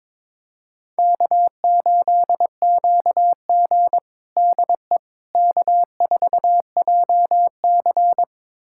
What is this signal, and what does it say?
Morse code, heard as a keyed tone, about 22 wpm, 700 Hz. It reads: K8QG DE K4JC
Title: USA Callsign DE Callsign - 4 Characters - 22wpm